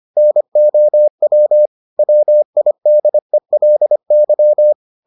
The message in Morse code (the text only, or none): NOW WIDELY